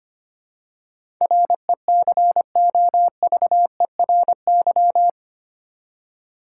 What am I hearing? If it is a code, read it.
Morse code, heard as a keyed tone, about 25 wpm, 690 Hz. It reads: RECOVERY
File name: 8 Letter Words - First 3 Letters Given - 25wpm